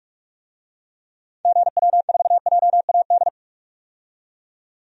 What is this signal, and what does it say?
Morse code, heard as a keyed tone, about 45 wpm, 700 Hz. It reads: GW4JAD
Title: DXCC-100 Call Signs - 6 Characters - 45wpm